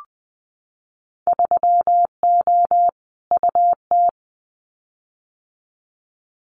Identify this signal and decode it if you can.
Morse code, heard as a keyed tone, about 20 wpm, 700 Hz. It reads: 3O UT
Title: Winter Field Day Exchanges 20wpm